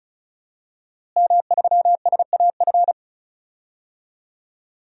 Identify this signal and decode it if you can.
Morse code, heard as a keyed tone, about 36 wpm, 700 Hz. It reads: M3SAF
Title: DXCC-100 Call Signs - 5 Characters - Repeated 3x - 35wpm